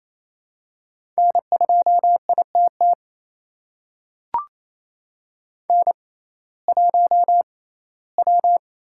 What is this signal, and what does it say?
Morse code, heard as a keyed tone, about 28 wpm, spaced as for 17 wpm, 700 Hz. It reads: N2ITT N1W
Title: USA Call Signs - 5 Characters - 6x Character Spacing - 28wpm